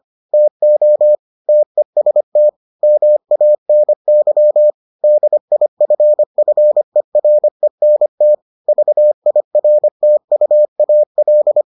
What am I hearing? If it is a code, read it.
Morse code, heard as a keyed tone, about 25 wpm, 600 Hz. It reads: TO TEST MANY DIFFERENT VIRTUAL